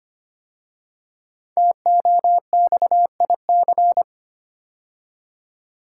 Morse code, heard as a keyed tone, about 25 wpm, 700 Hz. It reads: TOXIC
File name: 5 Letter Words - First 3 Letters Given - 25wpm